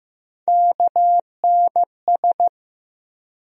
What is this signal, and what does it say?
Morse code, heard as a keyed tone, about 15 wpm, 700 Hz. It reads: KNS
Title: USA Call Sign Postfixes - 3 Letters 15wpm